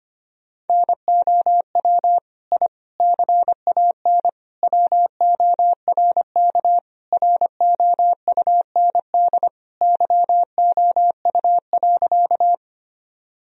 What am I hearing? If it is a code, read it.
Morse code, heard as a keyed tone, about 25 wpm, 700 Hz. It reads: NOW I CAN WORK ROUND YOU.
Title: Sentences from Top 1000 Words - Review Part 1 - 25wpm